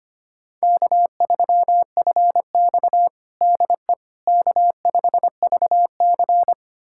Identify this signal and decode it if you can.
Morse code, heard as a keyed tone, about 25 wpm, 700 Hz. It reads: K3FX DE K5VC
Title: USA Callsign DE Callsign - 4 Characters - 25wpm